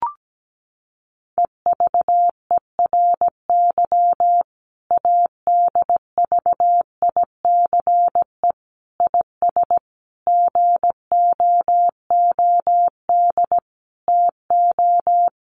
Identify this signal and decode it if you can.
Morse code, heard as a keyed tone, about 17 wpm, 700 Hz. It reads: EVERY ADVICE IS GOOD TO
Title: Sentences from Top 1100 Words 17wpm